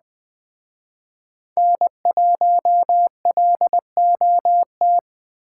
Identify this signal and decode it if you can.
Morse code, heard as a keyed tone, about 20 wpm, 700 Hz. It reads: N1LOT